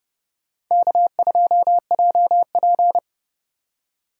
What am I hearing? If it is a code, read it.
Morse code, heard as a keyed tone, about 36 wpm, 700 Hz. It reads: K2JP
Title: Speed Racing - USA 1x2 Call Signs 30wpm